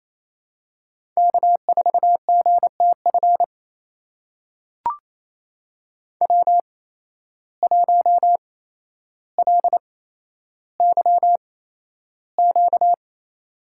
Morse code, heard as a keyed tone, about 28 wpm, spaced as for 12 wpm, 700 Hz. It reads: K4GTF W1LYQ
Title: USA Call Signs - 5 Characters - 8x Character Spacing - 28wpm